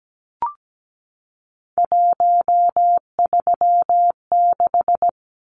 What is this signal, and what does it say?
Morse code, heard as a keyed tone, about 17 wpm, 700 Hz. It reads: 136
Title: Numbers - 3-Digit 17wpm